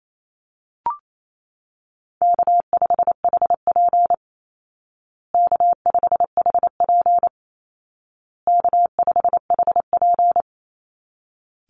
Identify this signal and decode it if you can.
Morse code, heard as a keyed tone, about 28 wpm, 700 Hz. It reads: K5HP K5HP K5HP
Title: Easy Call Signs Repeated 3x - 28wpm